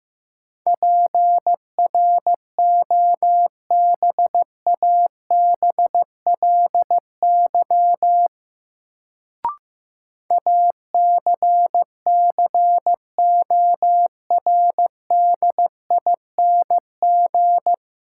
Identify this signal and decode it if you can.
Morse code, heard as a keyed tone, about 15 wpm, 700 Hz. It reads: PROBABLY ACCORDING
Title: Sentences from Top 700 Words - New Word Drill - 15wpm